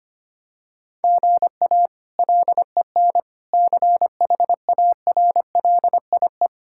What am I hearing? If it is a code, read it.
Morse code, heard as a keyed tone, about 25 wpm, 700 Hz. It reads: GA LEN CHARLIE
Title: K1USN SST QSO Practice - 25wpm